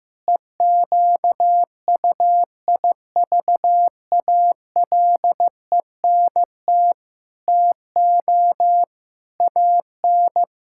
Morse code, heard as a keyed tone, about 15 wpm, 700 Hz. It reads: EQUIVALENT TO AN
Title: Sentences from Top 1600 Words 15wpm